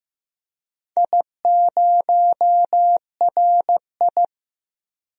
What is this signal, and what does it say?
Morse code, heard as a keyed tone, about 15 wpm, 700 Hz. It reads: I0RI
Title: DXCC-100 Call Signs - 4 Characters - Repeated 2x - 15wpm